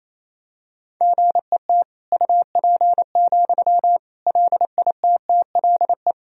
Morse code, heard as a keyed tone, about 28 wpm, 700 Hz. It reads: GET UP, LITTLE
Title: Sentences from Top 1000 Words - Review Part 1 - 28wpm